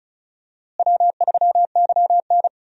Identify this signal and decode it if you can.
Morse code, heard as a keyed tone, about 36 wpm, 700 Hz. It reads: W3YN
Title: USA Call Signs - 4 Characters - Repeated 3x - 35wpm